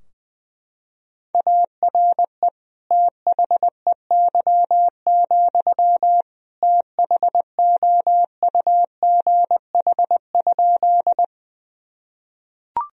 Morse code, heard as a keyed tone, about 20 wpm, 700 Hz. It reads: ARE THEY, THOUGH?
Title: Sentences from Top 1000 Words - Review Part 2 - 20wpm